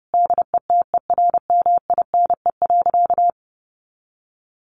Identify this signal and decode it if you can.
Morse code, heard as a keyed tone, about 28 wpm, 700 Hz. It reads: DETERMINE.
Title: Sentences from Top 1500 Words 30wpm